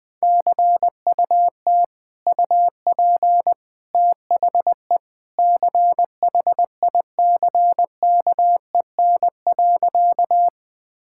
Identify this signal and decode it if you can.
Morse code, heard as a keyed tone, about 20 wpm, 700 Hz. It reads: CUT UP THE CHICKEN.
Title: Sentences from Top 1000 Words - Review Part 5 - 20wpm